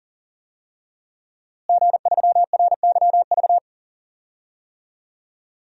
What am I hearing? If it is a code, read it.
Morse code, heard as a keyed tone, about 40 wpm, 700 Hz. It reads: G3RYV
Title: DXCC-100 Call Signs - 5 Characters - Repeated 2x - 40wpm